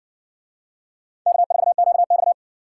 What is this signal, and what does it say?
Morse code, heard as a keyed tone, about 58 wpm, 700 Hz. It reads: K4QX